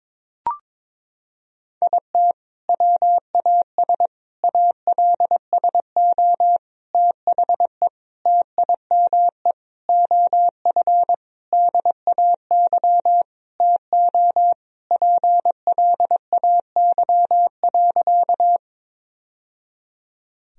Morse code, heard as a keyed tone, about 22 wpm, 700 Hz. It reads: IT WAS ALSO THE TIME OF DAY TO PLAY.
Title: Sentences from Top 1000 Words - Review Part 1 - 22wpm